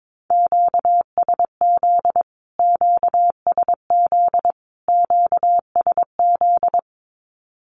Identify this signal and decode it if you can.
Morse code, heard as a keyed tone, about 22 wpm, 700 Hz. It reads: QSZ QSZ QSZ
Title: Q-Codes Repeated 3x 22wpm